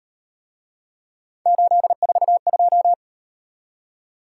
Morse code, heard as a keyed tone, about 38 wpm, 700 Hz. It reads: 842